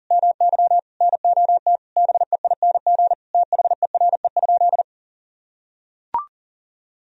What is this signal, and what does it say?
Morse code, heard as a keyed tone, about 40 wpm, 700 Hz. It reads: MY NOT BEING THERE?
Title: Sentences from Top 1000 Words - Review Part 3 - 40wpm